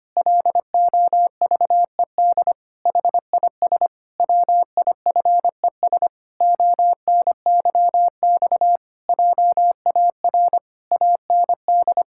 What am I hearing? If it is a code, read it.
Morse code, heard as a keyed tone, about 25 wpm, 700 Hz. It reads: LOVED HIS WIFES ONYX JAR AND
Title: Pangram Challenge v2 - 25wpm